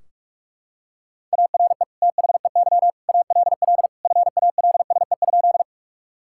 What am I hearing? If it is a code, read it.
Morse code, heard as a keyed tone, about 41 wpm, 700 Hz. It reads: ARE THEY ALL FALSE?
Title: Sentences from Top 1600 Words 45wpm